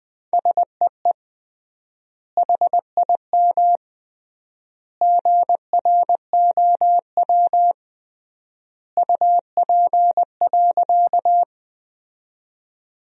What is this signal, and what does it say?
Morse code, heard as a keyed tone, about 20 wpm, 700 Hz. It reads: SEE HIM GROW UP.
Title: 3x Word-Spacing - Sentences from Top 300 Words - 20wpm